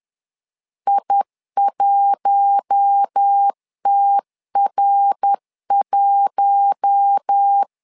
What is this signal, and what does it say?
Morse code, heard as a keyed tone, about 10 wpm, 800 Hz. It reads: I1TR1
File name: LEZIONI / 4 Q1 10wpm 2448